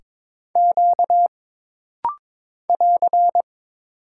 Morse code, heard as a keyed tone, about 22 wpm, 700 Hz. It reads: Q <AR>
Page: This rapid-fire timing is geared to push your ICR (Instant Character Recognition) skills.